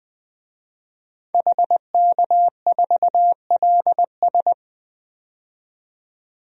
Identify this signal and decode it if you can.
Morse code, heard as a keyed tone, about 21 wpm, 700 Hz. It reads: HK4LS
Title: DXCC-100 Call Signs - 5 Characters - 20wpm